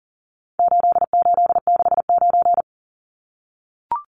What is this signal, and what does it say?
Morse code, heard as a keyed tone, about 40 wpm, 700 Hz. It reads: 8869
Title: Numbers - 4-Digit 40wpm